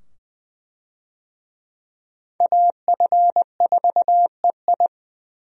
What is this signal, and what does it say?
Morse code, heard as a keyed tone, about 20 wpm, 700 Hz. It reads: AF4EI
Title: USA Call Signs - 5 Characters - 20wpm